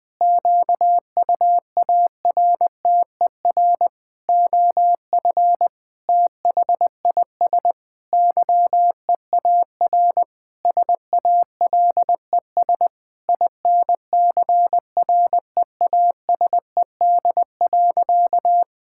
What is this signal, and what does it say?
Morse code, heard as a keyed tone, about 20 wpm, 700 Hz. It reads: QUARTER OF THIS YEAR SALES INCREASED.
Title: Sentences with 7-Word Phrases - 20wpm